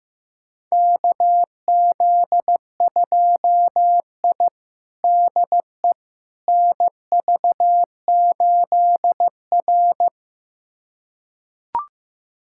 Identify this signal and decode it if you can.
Morse code, heard as a keyed tone, about 15 wpm, 700 Hz. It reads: KZ2I DE NV8R
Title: USA Callsign DE Callsign - 4 Characters - 15wpm